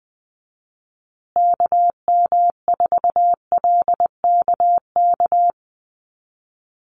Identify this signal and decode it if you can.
Morse code, heard as a keyed tone, about 20 wpm, 700 Hz. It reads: KM4LKK
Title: USA Call Signs Repeated 2x 20wpm